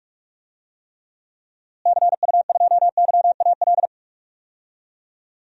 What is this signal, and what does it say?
Morse code, heard as a keyed tone, about 42 wpm, 700 Hz. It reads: CU2YAL
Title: DXCC-100 Call Signs - 6 Characters - Repeated 2x - 45wpm